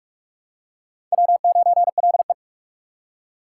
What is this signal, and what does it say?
Morse code, heard as a keyed tone, about 39 wpm, 700 Hz. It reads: W9LE